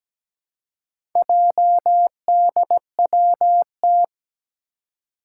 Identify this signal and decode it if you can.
Morse code, heard as a keyed tone, about 17 wpm, 700 Hz. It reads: JDWT
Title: Four Challenging Letter-Number Combinations 17wpm